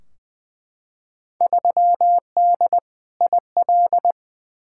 Morse code, heard as a keyed tone, about 20 wpm, 700 Hz. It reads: 3D IL
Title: Field Day Exchanges Encore - 20wpm